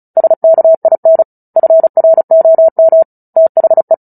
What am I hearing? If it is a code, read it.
Morse code, heard as a keyed tone, about 34 wpm, 650 Hz. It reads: SKIN FROM THE
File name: Book Two / 09-Chapter 9 - 35 WPM